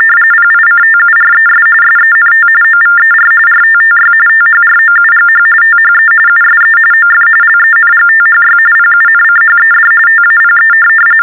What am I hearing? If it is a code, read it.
TTY/TDD text telephone tones, about 45.45 baud. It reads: PICTURES SENT OVER RTTY THROUGH THE USE OF LENGTHY PUNCHED TAPE
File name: teletype-RTTY-45